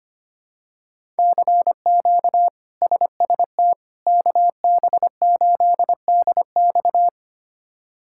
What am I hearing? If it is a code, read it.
Morse code, heard as a keyed tone, about 25 wpm, 700 Hz. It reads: CQ SST KB8DX
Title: K1USN SST QSO Practice - 25wpm